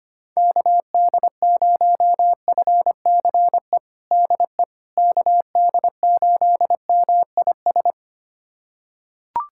Morse code, heard as a keyed tone, about 25 wpm, 700 Hz. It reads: KD0FCE DE KD8MIS
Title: USA Callsign DE Callsign 25wpm